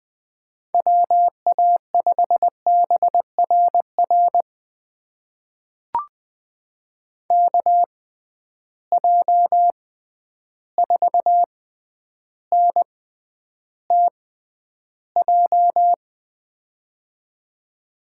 Morse code, heard as a keyed tone, about 20 wpm, spaced as for 10 wpm, 700 Hz. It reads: WA5BRR KJ4NTJ
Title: USA Call Signs - 6x Character Spacing - 20wpm